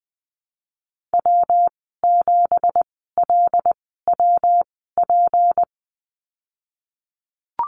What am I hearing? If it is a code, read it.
Morse code, heard as a keyed tone, about 20 wpm, spaced as for 15 wpm, 700 Hz. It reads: W7LWP
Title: 167 Lesson 27g - Number 7 - Call Signs 20wpm